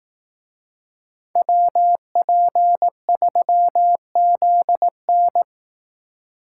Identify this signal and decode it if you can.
Morse code, heard as a keyed tone, about 18 wpm, 700 Hz. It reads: WP3ZN
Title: Speed Racing - USA Call Signs 15wpm